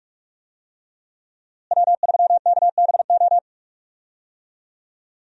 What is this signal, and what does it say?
Morse code, heard as a keyed tone, about 41 wpm, 700 Hz. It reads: W3KBO
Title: USA Call Signs - 5 Characters - Repeated 3x - 45wpm